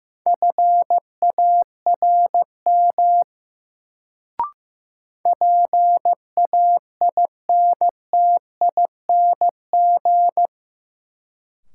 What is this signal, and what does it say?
Morse code, heard as a keyed tone, about 15 wpm, 700 Hz. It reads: FARM PAINTING
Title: Sentences from Top 1200 Words - New Word Drill Alt - 15wpm